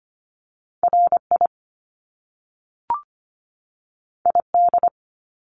Morse code, heard as a keyed tone, about 25 wpm, 700 Hz. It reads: RI ID
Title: Top Two Letter Combinations in Words 25wpm